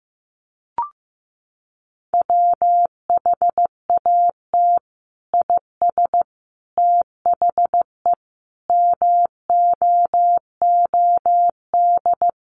Morse code, heard as a keyed tone, about 15 wpm, 700 Hz. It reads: WHAT IS THE MOOD
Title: Sentences from Top 1100 Words 15wpm